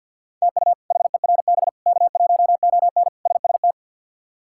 Morse code, heard as a keyed tone, about 52 wpm, 700 Hz. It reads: TU HERB K1ON SST
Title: K1USN SST QSO Practice - 50wpm